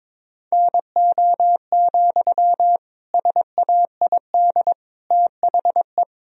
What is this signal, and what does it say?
Morse code, heard as a keyed tone, about 22 wpm, 700 Hz. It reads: NO, SAID THE
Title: Sentences from Top 1000 Words - Review Part 4 - 22wpm